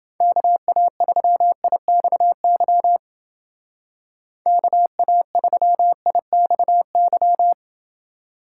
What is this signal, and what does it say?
Morse code, heard as a keyed tone, about 29 wpm, 700 Hz. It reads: KA3IXY KA3IXY